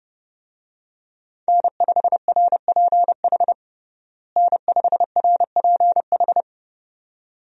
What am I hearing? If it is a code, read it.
Morse code, heard as a keyed tone, about 32 wpm, 700 Hz. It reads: N5RPH N5RPH
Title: USA Call Signs - 5 Characters - Repeated 2x - 30wpm